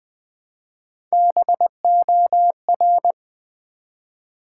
Morse code, heard as a keyed tone, about 20 wpm, 690 Hz. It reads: BOR
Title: Build A Word - 1 Letter At A Time - 5-Letter Words - 20wpm